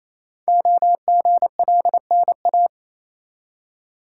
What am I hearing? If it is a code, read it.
Morse code, heard as a keyed tone, about 29 wpm, 700 Hz. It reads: OGLNA
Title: Code Groups - 5 Letters - 28wpm